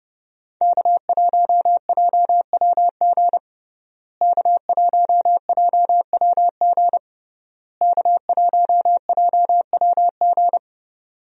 Morse code, heard as a keyed tone, about 30 wpm, 700 Hz. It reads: K1JWG K1JWG K1JWG
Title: USA Call Signs - 5 Characters - Repeated 3x - 30wpm